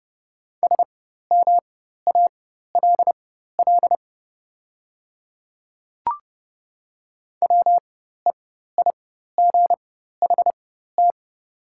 Morse code, heard as a keyed tone, about 30 wpm, spaced as for 14 wpm, 690 Hz. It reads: SMALL WEIGHT
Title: 4x Character Spacing - Top 500 Words - 30wpm